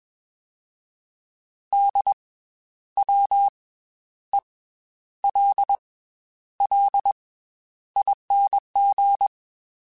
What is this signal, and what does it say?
Morse code, heard as a keyed tone, about 21 wpm, 800 Hz. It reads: D W E L L ING
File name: Special INGs / Special INGs-mp3s / Slow ing @20